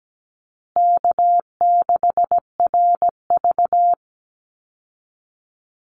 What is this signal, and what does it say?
Morse code, heard as a keyed tone, about 17 wpm, 700 Hz. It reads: K6RV
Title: Call Signs Repeated 2x - 17wpm